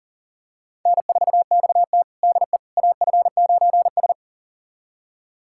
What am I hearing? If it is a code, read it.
Morse code, heard as a keyed tone, about 40 wpm, 700 Hz. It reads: N4XT DE AF9S
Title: USA Callsign DE Callsign - 4 Characters - 40wpm